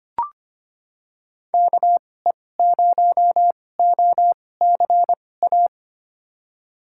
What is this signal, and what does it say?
Morse code, heard as a keyed tone, about 25 wpm, spaced as for 18 wpm, 700 Hz. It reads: KE0OCA
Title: USA Call Signs - 2x Character Spacing - 25wpm